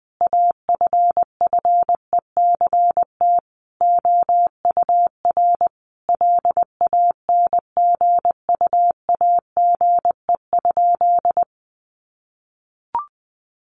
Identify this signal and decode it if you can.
Morse code, heard as a keyed tone, about 20 wpm, 700 Hz. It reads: AFFECT OUR LANGUAGE?
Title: Sentences from Top 900 Words 20wpm